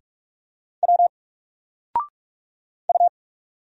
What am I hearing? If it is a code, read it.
Morse code, heard as a keyed tone, about 44 wpm, 700 Hz. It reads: W U
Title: Single Character (Letter:Number:Punctuation:Prosign) 45wpm